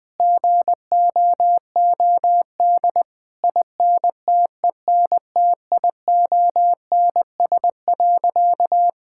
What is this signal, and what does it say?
Morse code, heard as a keyed tone, about 20 wpm, 700 Hz. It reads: GOOD INTENTIONS.
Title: Sentences with 8-Word Phrases - 20wpm